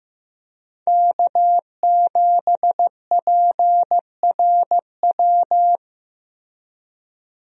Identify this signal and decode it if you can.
Morse code, heard as a keyed tone, about 15 wpm, 700 Hz. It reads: K7PRW
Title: USA Call Signs - 5 Characters - Repeated 2x - 15wpm